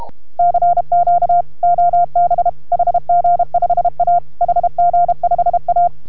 Morse code, heard as a keyed tone, about 32 wpm, 690 Hz. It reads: CQ OB HG5A HG5A